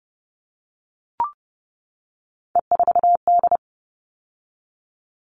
Morse code, heard as a keyed tone, about 30 wpm, 700 Hz. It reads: E4D
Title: All Cut Numbers - 3-Digit 30wpm